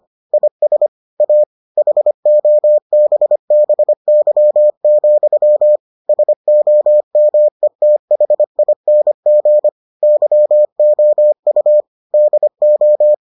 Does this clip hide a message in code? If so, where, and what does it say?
Yes: Morse code from 0.3 to 13.1 s, heard as a keyed tone, about 25 wpm, 600 Hz. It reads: IS A HOBBY, SOMETHING YOU DO